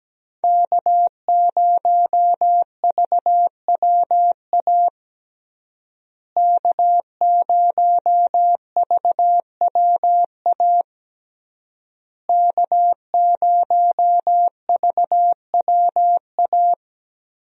Morse code, heard as a keyed tone, about 17 wpm, 700 Hz. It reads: K0VWA K0VWA K0VWA
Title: USA Call Signs - 5 Characters - Repeated 3x - 17wpm